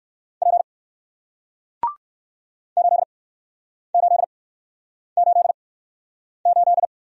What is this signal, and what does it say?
Morse code, heard as a keyed tone, about 61 wpm, spaced as for 9 wpm, 700 Hz. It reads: P 8888